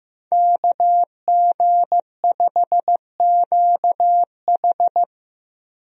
Morse code, heard as a keyed tone, about 15 wpm, 700 Hz. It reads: KG5QH
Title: USA Call Signs - 5 Characters - Repeated 3x - 15wpm